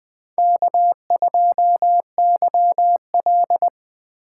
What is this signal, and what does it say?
Morse code, heard as a keyed tone, about 20 wpm, 700 Hz. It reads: K2YL
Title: USA Call Signs - 3x Character Spacing - 20wpm